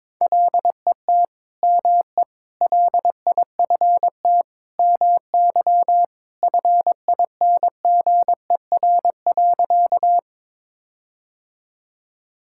Morse code, heard as a keyed tone, about 22 wpm, 700 Hz. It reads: LET ME LIFT MY FINGER.
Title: Sentences from Top 1600 Words 22wpm